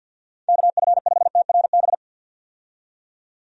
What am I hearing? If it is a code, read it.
Morse code, heard as a keyed tone, about 51 wpm, 700 Hz. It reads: KF5TRB
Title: USA Call Signs - 6 Characters - 50wpm